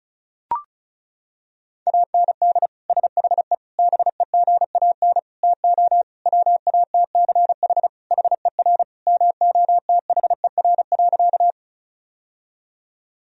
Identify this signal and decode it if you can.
Morse code, heard as a keyed tone, about 35 wpm, 700 Hz. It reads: AND SHE BEGAN TO WATCH HER MOTHER.
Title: Sentences from Top 1000 Words - Review Part 1 - 35wpm